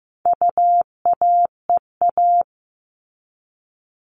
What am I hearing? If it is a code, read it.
Morse code, heard as a keyed tone, about 15 wpm, 700 Hz. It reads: UAEA
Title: All Cut Numbers - 4-Digit 15wpm